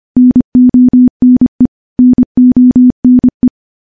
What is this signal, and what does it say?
Morse code, heard as a keyed tone, about 25 wpm, 250 Hz. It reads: NONE NONE
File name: morse / morse-training-25WPM